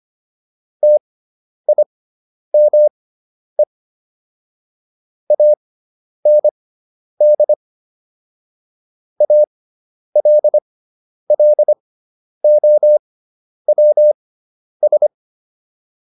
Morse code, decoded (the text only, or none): TIME AND ALLOWS